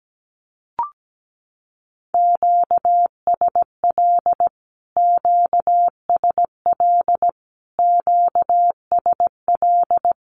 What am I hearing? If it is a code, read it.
Morse code, heard as a keyed tone, about 17 wpm, 700 Hz. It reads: QSL QSL QSL
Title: Q-Codes Repeated 3x 17wpm